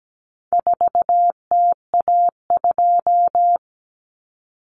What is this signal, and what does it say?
Morse code, heard as a keyed tone, about 17 wpm, 700 Hz. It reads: 4TA2
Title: Common Cut Numbers - 4-Digit 17wpm